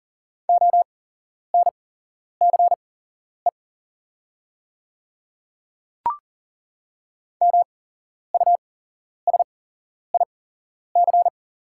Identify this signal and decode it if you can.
Morse code, heard as a keyed tone, about 40 wpm, spaced as for 11 wpm, 700 Hz. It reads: ONCE MUSIC
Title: 8x Character Spacing - Top 300 Words - 40wpm